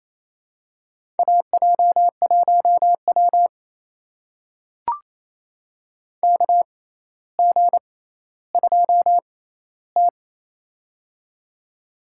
Morse code, heard as a keyed tone, about 28 wpm, spaced as for 14 wpm, 700 Hz. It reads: AJ1W KG2T
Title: USA Call Signs - 4 Characters - 6x Character Spacing - 28wpm